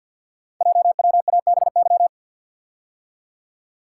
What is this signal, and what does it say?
Morse code, heard as a keyed tone, about 46 wpm, 700 Hz. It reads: JWABY
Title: Code Groups - 5 Letters - 50wpm